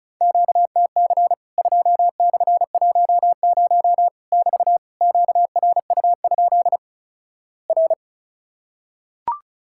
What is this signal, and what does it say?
Morse code, heard as a keyed tone, about 36 wpm, 700 Hz. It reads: QTC 2/10 <BT> QRU? R
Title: Worked All Europe DX Contest - QTC Exchange - 35wpm